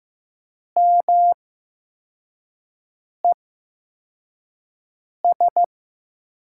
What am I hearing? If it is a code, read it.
Morse code, heard as a keyed tone, about 15 wpm, spaced as for 4 wpm, 700 Hz. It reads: MES
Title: USA Call Signs - 6 Characters - 8x Character Spacing - 15wpm